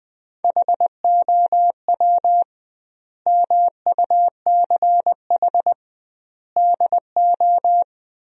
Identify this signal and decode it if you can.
Morse code, heard as a keyed tone, about 20 wpm, 700 Hz. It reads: HOW MUCH DO
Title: 2x Word-Spacing - Sentences from Top 200 Words - 20wpm